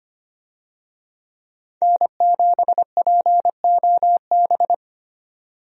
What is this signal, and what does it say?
Morse code, heard as a keyed tone, about 25 wpm, 700 Hz. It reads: N7POB
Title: USA Call Signs - 5 Characters - 25wpm